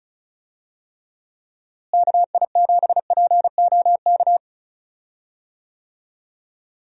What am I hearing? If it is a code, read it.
Morse code, heard as a keyed tone, about 36 wpm, 700 Hz. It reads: KI7POK
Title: USA Call Signs - 6 Characters - Repeated 3x - 35wpm